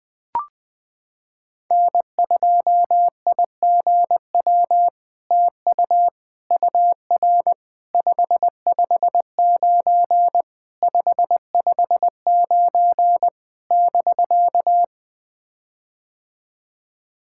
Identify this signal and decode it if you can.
Morse code, heard as a keyed tone, about 20 wpm, 700 Hz. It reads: N2IGW TU UR 559 559 <BK>
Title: POTA QSO Practice - 20wpm